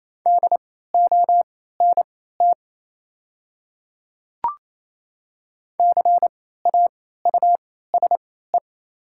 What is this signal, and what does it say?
Morse code, heard as a keyed tone, about 28 wpm, spaced as for 16 wpm, 700 Hz. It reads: DONT CAUSE
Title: 3x Character Spacing - Top 300 Words - 28wpm